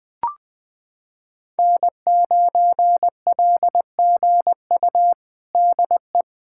Morse code, heard as a keyed tone, about 21 wpm, 700 Hz. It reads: N9LGU DE
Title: USA Callsign DE Callsign - 5 Characters - 20wpm